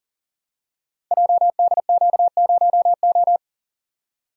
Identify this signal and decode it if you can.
Morse code, heard as a keyed tone, about 40 wpm, 700 Hz. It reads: JDQ0O
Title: Code Groups - 5 Letter:Numbers - 40wpm